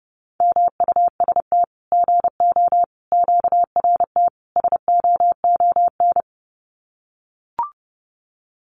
Morse code, heard as a keyed tone, about 30 wpm, 700 Hz. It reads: MUST GO QRT SOON
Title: Common QSO Phrases 30wpm